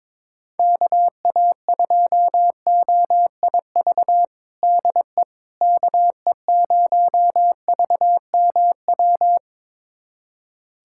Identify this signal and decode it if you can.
Morse code, heard as a keyed tone, about 21 wpm, 700 Hz. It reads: KA2OIV DE KE0VMW
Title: USA Callsign DE Callsign - 6 Characters - 22wpm